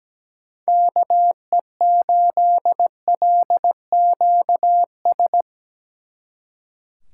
Morse code, heard as a keyed tone, about 20 wpm, 700 Hz. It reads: KE8LQS
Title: Speed Racing - USA 2x3 Call Signs 17wpm